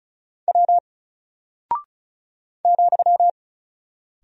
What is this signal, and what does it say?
Morse code, heard as a keyed tone, about 34 wpm, 700 Hz. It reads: W ,
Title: Single Character (Letter:Number:Punctuation:Prosign) Alt 35wpm